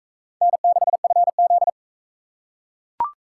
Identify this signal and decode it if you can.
Morse code, heard as a keyed tone, about 40 wpm, 700 Hz. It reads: N6FZ